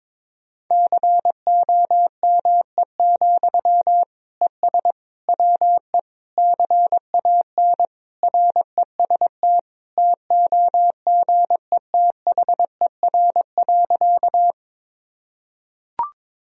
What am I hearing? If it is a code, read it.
Morse code, heard as a keyed tone, about 22 wpm, 700 Hz. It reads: COME, ES WE CAN REST TOGETHER.
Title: Sentences from Top 500 Words and Abbreviations 22wpm